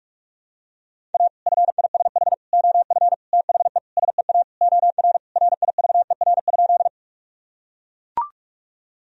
Morse code, heard as a keyed tone, about 41 wpm, 700 Hz. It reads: A FISH OF THE SEA OR RIVER?
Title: Sentences from Top 1000 Words - Review Part 1 - 45wpm